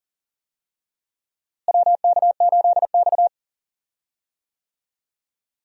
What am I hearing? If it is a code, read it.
Morse code, heard as a keyed tone, about 40 wpm, 700 Hz. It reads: WK8X
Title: USA Call Signs - 4 Characters - Repeated 3x - 40wpm